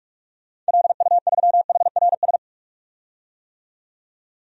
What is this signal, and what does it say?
Morse code, heard as a keyed tone, about 46 wpm, 700 Hz. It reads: LU3HRS
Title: DXCC-100 Call Signs - 6 Characters - 45wpm